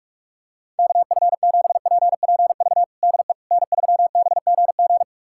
Morse code, heard as a keyed tone, about 41 wpm, 700 Hz. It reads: KF7PPV DE N3BGG
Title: USA Callsign DE Callsign 45wpm